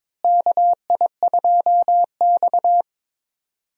Morse code, heard as a keyed tone, about 22 wpm, 700 Hz. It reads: KI2X
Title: USA Call Signs - 4 Characters - Repeated 3x - 22wpm